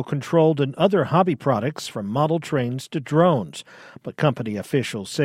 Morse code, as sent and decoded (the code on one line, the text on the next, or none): none